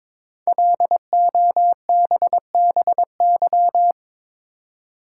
Morse code, heard as a keyed tone, about 22 wpm, 700 Hz. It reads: LOBBY
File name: 5 Letter Words - First 3 Letters Given - 22wpm